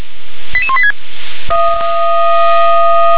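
Five-tone selective call (ZVEI), heard as quick runs of five tones, tones of about 70 ms. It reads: 88186
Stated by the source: ZVEI five-tone selective call received by SDR, code 88161.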